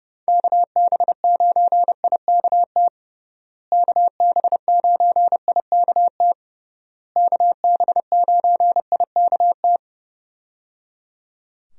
Morse code, heard as a keyed tone, about 30 wpm, 700 Hz. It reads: KB9IKT KB9IKT KB9IKT
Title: USA Call Signs - 6 Characters - Repeated 3x - 30wpm